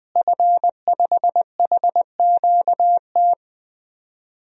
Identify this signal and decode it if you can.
Morse code, heard as a keyed tone, about 21 wpm, 700 Hz. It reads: F5HQT
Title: DXCC-100 Call Signs - 5 Characters - Repeated 2x - 20wpm